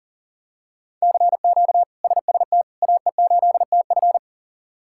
Morse code, heard as a keyed tone, about 40 wpm, 700 Hz. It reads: CQ SST AE8TF
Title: K1USN SST QSO Practice - 40wpm